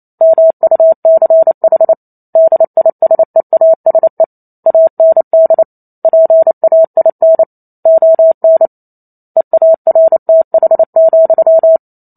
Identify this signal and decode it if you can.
Morse code, heard as a keyed tone, about 28 wpm, 650 Hz. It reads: MUCH DISEASE AND PAIN ON EARTH,